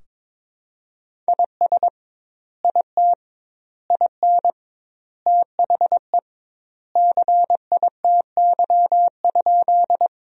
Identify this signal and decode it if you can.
Morse code, heard as a keyed tone, about 22 wpm, 700 Hz. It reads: IS IT IN THE CITY?
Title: 2x Word-Spacing - Sentences from Top 300 Words - 22wpm